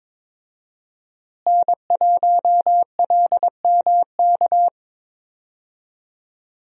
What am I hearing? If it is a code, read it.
Morse code, heard as a keyed tone, about 22 wpm, 700 Hz. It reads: N1LMK
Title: USA Call Signs - 5 Characters - Repeated 2x - 22wpm